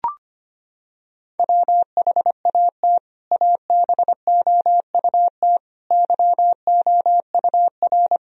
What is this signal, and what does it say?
Morse code, heard as a keyed tone, about 25 wpm, 700 Hz. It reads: WHAT ABOUT YOUR
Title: Sentences from Top 1000 Words - Review Part 1 - 25wpm